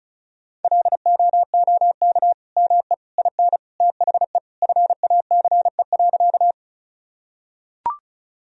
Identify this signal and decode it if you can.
Morse code, heard as a keyed tone, about 35 wpm, 700 Hz. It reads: LOOK ME IN THE FACE.
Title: Sentences from Top 1000 Words - Review Part 2 - 35wpm